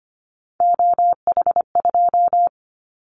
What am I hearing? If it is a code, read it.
Morse code, heard as a keyed tone, about 25 wpm, 700 Hz. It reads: OH2
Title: Callsign Prefixes for Top 25 DXCC - Repeated 3x - 25wpm